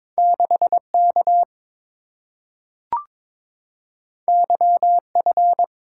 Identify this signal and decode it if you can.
Morse code, heard as a keyed tone, about 22 wpm, 700 Hz. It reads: 6K YF
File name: practice / 22wpm / Two Letter-Number Combinations - 22wpm